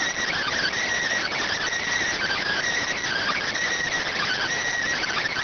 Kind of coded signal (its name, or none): Emergency Alert System (SAME) tones